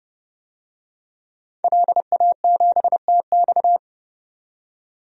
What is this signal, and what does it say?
Morse code, heard as a keyed tone, about 30 wpm, 700 Hz. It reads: LA7TX
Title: DXCC-100 Call Signs - 5 Characters - 30wpm